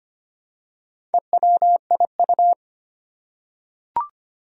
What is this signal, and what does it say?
Morse code, heard as a keyed tone, about 25 wpm, 700 Hz. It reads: EWIU
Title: Four Challenging Letter-Number Combinations 25wpm